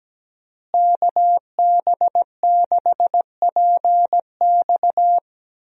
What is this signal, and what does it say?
Morse code, heard as a keyed tone, about 17 wpm, 700 Hz. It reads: KB6PX
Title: USA Call Signs - 5 Characters - Repeated 3x - 17wpm